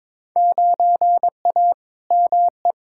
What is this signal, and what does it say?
Morse code, heard as a keyed tone, about 22 wpm, 700 Hz. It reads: 9A ME
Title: Field Day Exchanges Encore - 22wpm